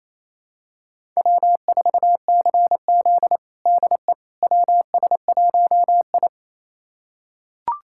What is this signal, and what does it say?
Morse code, heard as a keyed tone, about 28 wpm, 700 Hz. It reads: W4CZ DE WS1I
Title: USA Callsign DE Callsign - 4 Characters - 28wpm